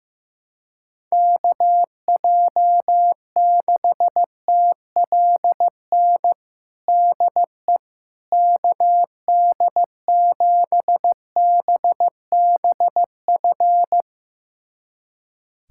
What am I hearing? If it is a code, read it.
Morse code, heard as a keyed tone, about 15 wpm, 700 Hz. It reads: KJ6TLN DE KD7BBF
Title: USA Callsign DE Callsign - 6 Characters - 15wpm